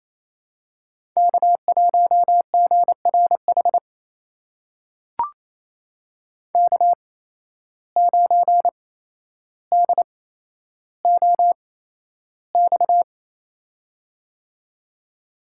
Morse code, heard as a keyed tone, about 28 wpm, spaced as for 12 wpm, 700 Hz. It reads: K1GRH K9DOX
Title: USA Call Signs - 5 Characters - 8x Character Spacing - 28wpm